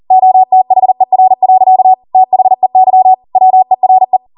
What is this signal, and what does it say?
Morse code, heard as a keyed tone, about 40 wpm, 750 Hz. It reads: OTHER. THEY WERE